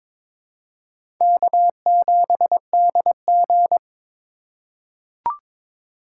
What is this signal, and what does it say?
Morse code, heard as a keyed tone, about 22 wpm, 700 Hz. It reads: K7DG
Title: USA Call Signs - 4 Characters - 4x Character Spacing - 22wpm